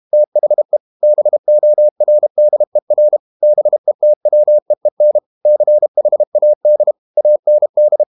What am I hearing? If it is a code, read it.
Morse code, heard as a keyed tone, about 32 wpm, 600 Hz. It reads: THE BORDER BETWEEN CHAD AND
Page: Never 31wpm | CW med Gnister